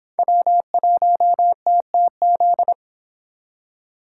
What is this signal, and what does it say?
Morse code, heard as a keyed tone, about 30 wpm, 700 Hz. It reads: W1TTZ